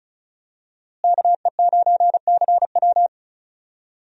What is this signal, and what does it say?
Morse code, heard as a keyed tone, about 34 wpm, 700 Hz. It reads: KE9CW
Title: USA Call Signs - 5 Characters - Repeated 3x - 35wpm